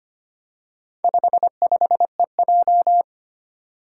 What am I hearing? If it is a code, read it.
Morse code, heard as a keyed tone, about 25 wpm, 700 Hz. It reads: 55EJ
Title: Four Challenging Letter-Number Combinations 25wpm